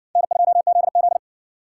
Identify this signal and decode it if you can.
Morse code, heard as a keyed tone, about 60 wpm, 700 Hz. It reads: N2ZZ